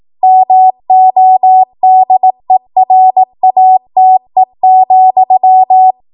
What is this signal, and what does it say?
Morse code, heard as a keyed tone, about 18 wpm, 750 Hz. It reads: MODERATE,
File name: Morse / 18WPM